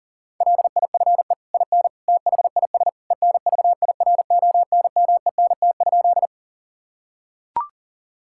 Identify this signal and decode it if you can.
Morse code, heard as a keyed tone, about 40 wpm, 700 Hz. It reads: LIFE IN THIS ENVIRONMENT?
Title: Sentences from Top 1000 Words - Review Part 5 - 40wpm